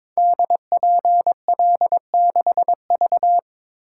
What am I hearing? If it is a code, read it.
Morse code, heard as a keyed tone, about 22 wpm, 690 Hz. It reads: DPL6V
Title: Code Groups - 5 Letter:Numbers - 22wpm